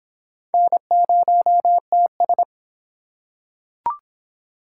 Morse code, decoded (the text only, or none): N0TS